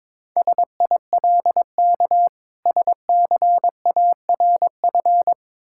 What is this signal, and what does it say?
Morse code, heard as a keyed tone, about 22 wpm, 690 Hz. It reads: SILK SCARF
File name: Difficult Words with Hints - 22wpm